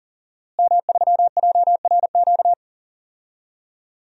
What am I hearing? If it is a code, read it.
Morse code, heard as a keyed tone, about 40 wpm, 700 Hz. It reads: M3JRQ
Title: DXCC-100 Call Signs - 5 Characters - Repeated 2x - 40wpm